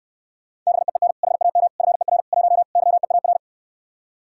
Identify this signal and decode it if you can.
Morse code, heard as a keyed tone, about 68 wpm, 700 Hz. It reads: BEEN HAM FER 1 YEAR